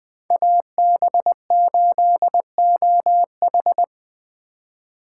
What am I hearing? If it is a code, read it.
Morse code, heard as a keyed tone, about 20 wpm, 700 Hz. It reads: AB8OH